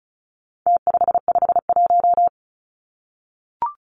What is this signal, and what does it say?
Morse code, heard as a keyed tone, about 34 wpm, 700 Hz. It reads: T551
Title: Numbers - Contest Style - 4-Digit 35wpm